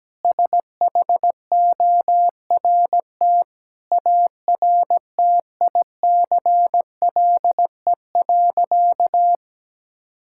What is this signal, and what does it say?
Morse code, heard as a keyed tone, about 17 wpm, 700 Hz. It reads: SHORT ARTICLE.
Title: Sentences from Top 1000 Words - Review Part 5 - 17wpm